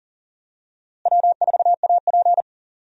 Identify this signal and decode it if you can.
Morse code, heard as a keyed tone, about 40 wpm, 700 Hz. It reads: W4AP
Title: USA Call Signs - 4 Characters - Repeated 3x - 40wpm